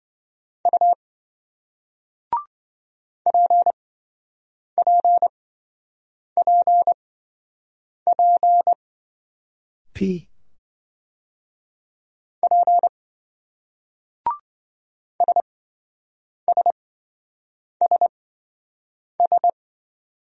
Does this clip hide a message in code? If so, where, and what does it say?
Yes: Morse code from 0.7 to 19.5 s, heard as a keyed tone, about 27 wpm, spaced as for 7 wpm, 700 Hz. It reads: U PPPP P SSSS